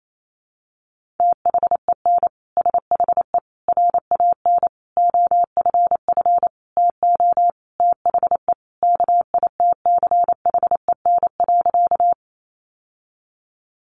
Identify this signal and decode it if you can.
Morse code, heard as a keyed tone, about 28 wpm, 700 Hz. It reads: THEN SHE RAN OFF TO THE KITCHEN.
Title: Sentences from Top 1400 Words 28wpm